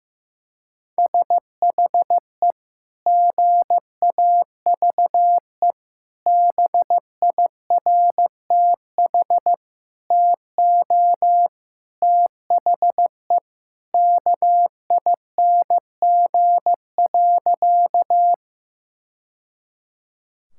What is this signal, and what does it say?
Morse code, heard as a keyed tone, about 15 wpm, 700 Hz. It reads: SHE GAVE BIRTH TO THE KING.
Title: Sentences from Top 1000 Words - Review Part 5 - 15wpm